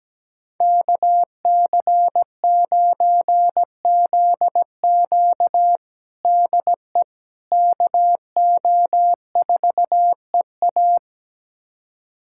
Morse code, heard as a keyed tone, about 17 wpm, 700 Hz. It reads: KC9ZQ DE KO4EA
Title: USA Callsign DE Callsign - 5 Characters - 17wpm